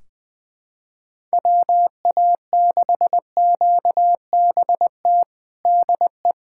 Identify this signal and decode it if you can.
Morse code, heard as a keyed tone, about 20 wpm, 700 Hz. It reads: WA6QBT DE
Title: USA Callsign DE Callsign 20wpm